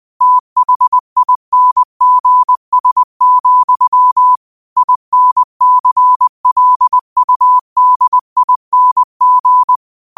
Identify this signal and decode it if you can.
Morse code, heard as a keyed tone, about 20 wpm, 1000 Hz. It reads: THINGS, INCLUDING
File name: 20wpm